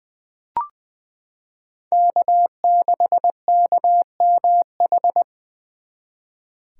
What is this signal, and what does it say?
Morse code, heard as a keyed tone, about 20 wpm, 700 Hz. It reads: K6KMH
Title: 2023 SOTA Chaser Call Signs - 20wpm